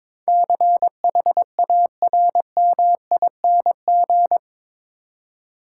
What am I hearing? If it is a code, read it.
Morse code, heard as a keyed tone, about 22 wpm, 700 Hz. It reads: CHARMING
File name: practice / 22wpm / Eight Letter Words - 22wpm